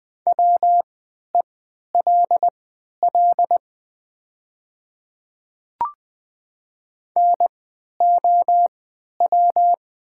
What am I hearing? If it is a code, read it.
Morse code, heard as a keyed tone, about 20 wpm, spaced as for 11 wpm, 700 Hz. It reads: WELL NOW
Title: 3x Character Spacing - Top 200 Words - 20wpm